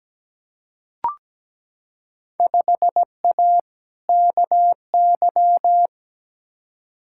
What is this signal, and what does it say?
Morse code, heard as a keyed tone, about 17 wpm, 700 Hz. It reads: 5A KY
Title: Field Day Exchanges - 17wpm